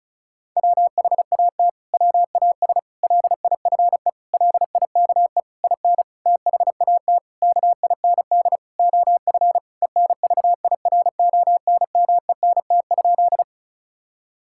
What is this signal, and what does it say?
Morse code, heard as a keyed tone, about 36 wpm, 700 Hz. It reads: WHAT WAS LIFE LIKE IN THAT KIND OF ENVIRONMENT?
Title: Sentences from Top 1000 Words - Review Part 5 - 35wpm